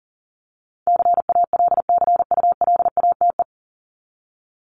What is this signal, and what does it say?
Morse code, heard as a keyed tone, about 40 wpm, 700 Hz. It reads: CALCULATE
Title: Top 1751-2000 Words 40wpm